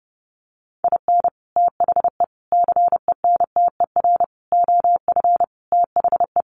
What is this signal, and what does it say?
Morse code, heard as a keyed tone, about 30 wpm, 700 Hz. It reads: IN THE CENTER OF THE
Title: Sets of 5 Words 30wpm